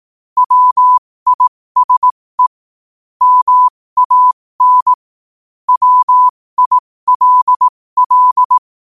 Morse code, heard as a keyed tone, about 18 wpm, spaced as for 15 wpm, 1000 Hz. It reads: WISE MAN WILL
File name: qotdfiles / 15wpm